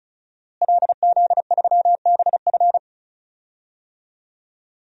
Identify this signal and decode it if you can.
Morse code, heard as a keyed tone, about 36 wpm, 700 Hz. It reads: LZ3BF
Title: DXCC-100 Call Signs - 5 Characters - Repeated 3x - 35wpm